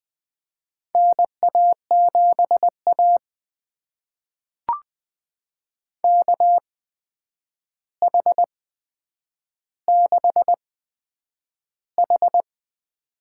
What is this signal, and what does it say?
Morse code, heard as a keyed tone, about 20 wpm, spaced as for 9 wpm, 700 Hz. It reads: NA7A KH6H
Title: USA Call Signs - 4 Characters - 8x Character Spacing - 20wpm